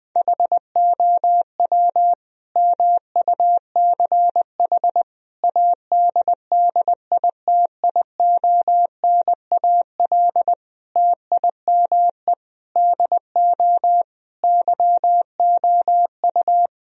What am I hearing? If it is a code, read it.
Morse code, heard as a keyed tone, about 21 wpm, 700 Hz. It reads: HOW MUCH ADDITIONAL TIME DO YOU
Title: Sentences from Top 1000 Words - Review Part 4 - 20wpm